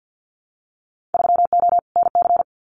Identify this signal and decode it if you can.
Morse code, heard as a keyed tone, about 50 wpm, 700 Hz. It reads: 3O NC
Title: Winter Field Day Exchanges 50wpm